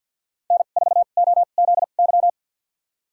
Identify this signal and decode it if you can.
Morse code, heard as a keyed tone, about 52 wpm, spaced as for 37 wpm, 700 Hz. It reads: N4OZY
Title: USA Call Signs - 5 Characters - 2x Character Spacing - 50wpm